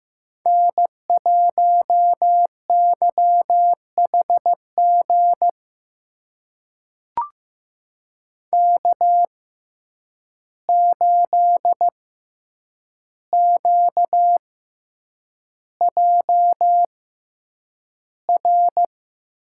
Morse code, heard as a keyed tone, about 15 wpm, spaced as for 8 wpm, 700 Hz. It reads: N1YHG K8QJR
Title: USA Call Signs - 5 Characters - 6x Character Spacing - 15wpm